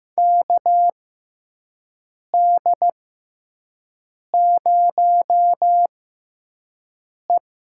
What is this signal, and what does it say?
Morse code, heard as a keyed tone, about 15 wpm, spaced as for 5 wpm, 700 Hz. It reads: KD0E
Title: USA Call Signs - 6x Character Spacing - 15wpm